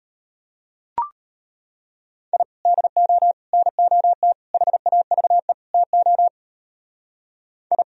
Morse code, heard as a keyed tone, about 32 wpm, 700 Hz. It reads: I DO NOT HAVE TO I